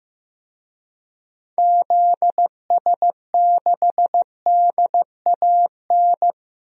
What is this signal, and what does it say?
Morse code, heard as a keyed tone, about 15 wpm, 700 Hz. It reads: ZS6DAN
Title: DXCC-100 Call Signs - 6 Characters - Repeated 3x - 15wpm